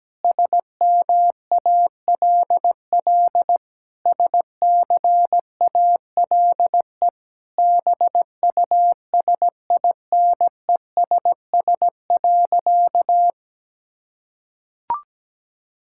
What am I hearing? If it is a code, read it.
Morse code, heard as a keyed tone, about 17 wpm, 700 Hz. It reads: SMALL SCALE BUSINESS.
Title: Sentences from Top 1000 Words - Review Part 5 - 17wpm